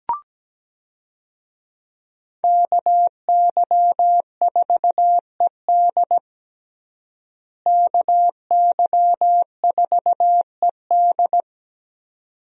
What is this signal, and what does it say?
Morse code, heard as a keyed tone, about 17 wpm, 700 Hz. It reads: KY4ED KY4ED
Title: USA Call Signs - 5 Characters - Repeated 2x - 17wpm